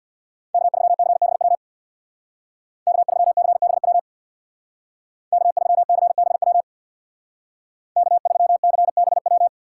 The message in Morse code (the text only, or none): K3CBW K3CBW K3CBW K3CBW